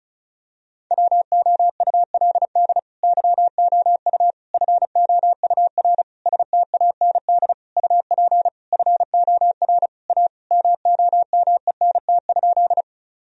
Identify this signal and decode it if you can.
Morse code, heard as a keyed tone, about 36 wpm, 700 Hz. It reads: WOULD YOU FOUR STAND UP FOR A MOMENT?
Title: Sentences from Top 1000 Words - Review Part 1 - 35wpm